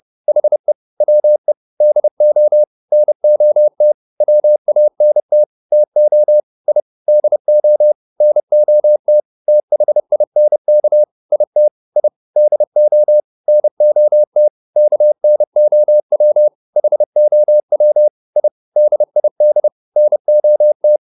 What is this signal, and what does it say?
Morse code, heard as a keyed tone, about 30 wpm, 600 Hz. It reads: HE WE DO NOT WANT TO I DO NOT THINK IT I DO NOT KNOW HOW I DID NOT